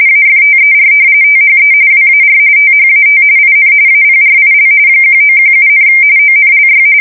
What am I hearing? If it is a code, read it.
RTTY, about 45.45 baud. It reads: AFTER THE WAR THAT IN APRIL 1917, ONLY
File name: fsk170 / usnavy4